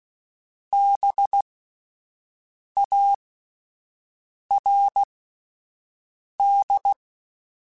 Morse code, heard as a keyed tone, about 16 wpm, spaced as for 5 wpm, 800 Hz. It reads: BARD